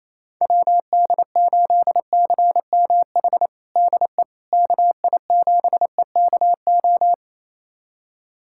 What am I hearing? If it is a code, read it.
Morse code, heard as a keyed tone, about 28 wpm, 700 Hz. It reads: WD8CMH DE KI7EKO
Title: USA Callsign DE Callsign 28wpm